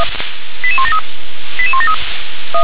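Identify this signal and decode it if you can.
Five-tone selective call (ZVEI), heard as quick runs of five tones, tones of about 70 ms. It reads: 88173 88173
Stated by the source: ZVEI five-tone call received via SDR.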